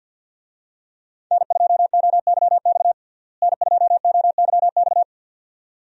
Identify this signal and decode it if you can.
Morse code, heard as a keyed tone, about 51 wpm, 700 Hz. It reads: N2OYX N2OYX
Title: USA Call Signs - 5 Characters - Repeated 2x - 50wpm